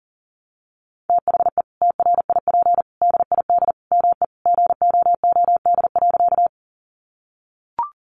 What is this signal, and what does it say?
Morse code, heard as a keyed tone, about 40 wpm, 700 Hz. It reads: THE TRIP DID ME GOOD.
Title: Sentences from Top 1300 Words 40wpm